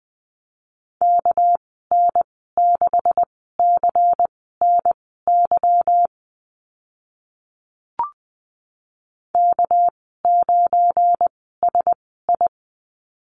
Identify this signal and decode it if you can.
Morse code, heard as a keyed tone, about 20 wpm, spaced as for 14 wpm, 700 Hz. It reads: KN6CNY K9SI
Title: 213 Lesson 34g - Number 8 - Call Signs 20wpm